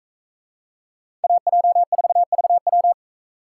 Morse code, heard as a keyed tone, about 42 wpm, 700 Hz. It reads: AJ4VW